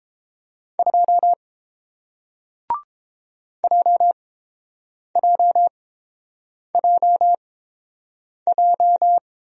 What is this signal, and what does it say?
Morse code, heard as a keyed tone, about 30 wpm, spaced as for 7 wpm, 700 Hz. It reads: 2 JJJJ